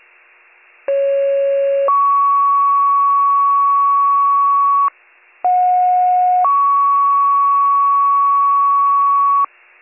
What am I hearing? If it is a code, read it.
Two-tone page, heard as a one-second tone, then a three-second tone, about 550 then 1120 Hz, about 730 then 1120 Hz. It is heard from 0.9 to 9.4 s.